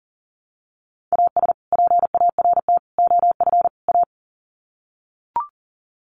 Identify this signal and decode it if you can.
Morse code, heard as a keyed tone, about 38 wpm, 700 Hz. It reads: AS PART OF A
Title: Sets of 4 Words 40wpm